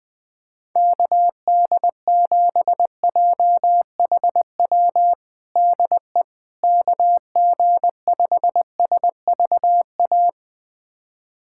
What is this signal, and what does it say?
Morse code, heard as a keyed tone, about 20 wpm, 700 Hz. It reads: KD7JHW DE KG5SVA
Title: USA Callsign DE Callsign - 6 Characters - 20wpm